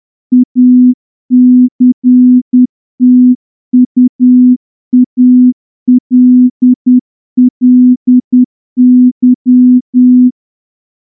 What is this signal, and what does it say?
Morse code, heard as a keyed tone, about 10 wpm, 250 Hz. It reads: ACTUALLY
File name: morse-training-10WPM